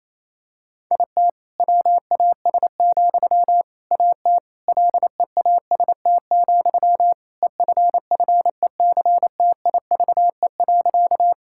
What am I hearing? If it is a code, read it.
Morse code, heard as a keyed tone, about 28 wpm, 700 Hz. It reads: IT WAS, AT LEAST, EFFECTIVE.
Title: Sentences from Top 1000 Words - Review Part 6 - 28wpm